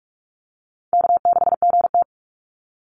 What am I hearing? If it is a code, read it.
Morse code, heard as a keyed tone, about 44 wpm, 700 Hz. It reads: K6GT
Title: Call Signs - 45wpm